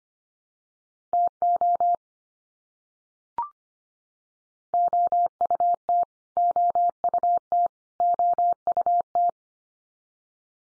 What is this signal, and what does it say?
Morse code, heard as a keyed tone, about 25 wpm, 700 Hz. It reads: TO OUT OUT OUT
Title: Top 100 Words 25wpm x3